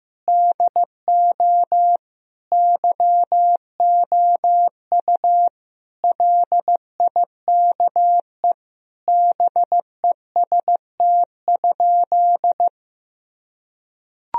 Morse code, heard as a keyed tone, about 15 wpm, 700 Hz. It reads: DO YOU LIKE BEST?
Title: Sentences from Top 1000 Words - Review Part 2 - 15wpm